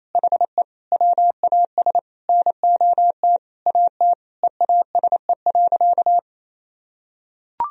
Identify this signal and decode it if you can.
Morse code, heard as a keyed tone, about 28 wpm, 700 Hz. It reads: HE WAS NOT AT EASE.
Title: Sentences from Top 1000 Words - Review Part 1 - 28wpm